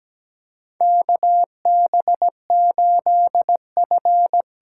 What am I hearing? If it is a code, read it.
Morse code, heard as a keyed tone, about 17 wpm, 700 Hz. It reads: KB8F
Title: USA Call Signs - 6 Characters - 8x Character Spacing - 17wpm